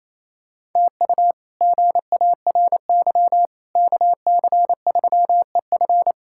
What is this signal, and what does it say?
Morse code, heard as a keyed tone, about 28 wpm, 700 Hz. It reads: TU GARY KC3EF
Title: K1USN SST QSO Practice - 28wpm